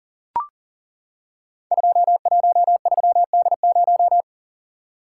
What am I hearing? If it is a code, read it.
Morse code, heard as a keyed tone, about 40 wpm, 690 Hz. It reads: 213D0
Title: Code Groups - 5 Letter:Numbers - 40wpm